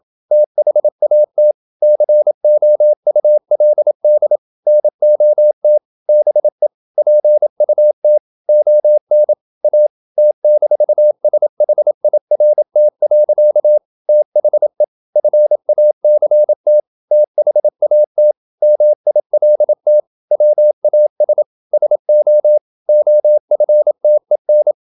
Morse code, heard as a keyed tone, about 27 wpm, 600 Hz. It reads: THAT COULD NOT BE PUT ON A T-SHIRT. THE FACT THAT MILT WAS SO OFTEN
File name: Never 0027wpm